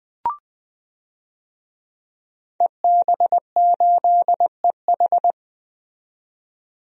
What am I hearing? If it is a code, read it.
Morse code, heard as a keyed tone, about 21 wpm, 700 Hz. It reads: EB8EH
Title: DXCC-100 Call Signs - 5 Characters - 20wpm